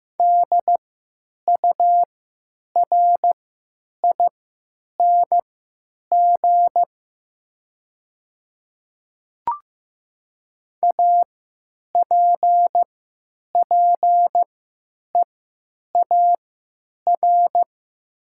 Morse code, heard as a keyed tone, about 15 wpm, spaced as for 9 wpm, 700 Hz. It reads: DURING APPEAR
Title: 3x Character Spacing - Top 500 Words - 15wpm